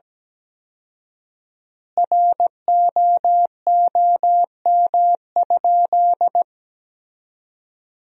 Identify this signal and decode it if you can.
Morse code, heard as a keyed tone, about 17 wpm, 700 Hz. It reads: ROOM?
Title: 4x Word-Spacing - Sentences from Top 300 Words - 17wpm